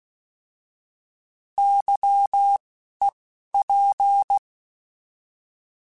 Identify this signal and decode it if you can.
Morse code, heard as a keyed tone, about 16 wpm, 800 Hz. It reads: Y E P
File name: Misc / Misc-mp3s / Misc @10